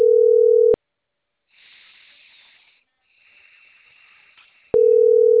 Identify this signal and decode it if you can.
Ringback tone (North America), 440 + 480 Hz.